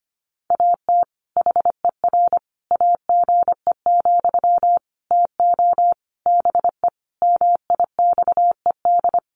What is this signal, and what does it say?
Morse code, heard as a keyed tone, about 25 wpm, 700 Hz. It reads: AT HER AGE, TO BE MIXED
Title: Sentences from Top 1500 Words 25wpm